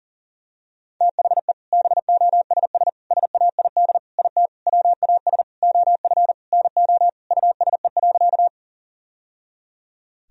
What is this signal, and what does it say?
Morse code, heard as a keyed tone, about 40 wpm, 700 Hz. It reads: THE BOSS SAID IT WAS OF NO USE.
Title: Sentences from Top 1000 Words - Review Part 4 - 40wpm